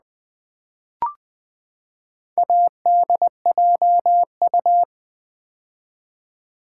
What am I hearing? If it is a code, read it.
Morse code, heard as a keyed tone, about 19 wpm, 700 Hz. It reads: ADJU
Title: Build A Word - 4 Letters At A Time - 10-Letter Words - 20wpm